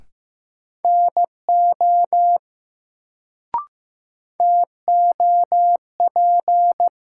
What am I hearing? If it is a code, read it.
Morse code, heard as a keyed tone, about 15 wpm, 700 Hz. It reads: NO TOP
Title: Sets of 2 Words - New Word Drill - 15wpm